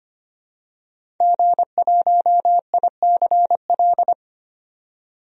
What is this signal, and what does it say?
Morse code, heard as a keyed tone, about 25 wpm, 700 Hz. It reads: G1ICL
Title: DXCC-100 Call Signs - 5 Characters - 25wpm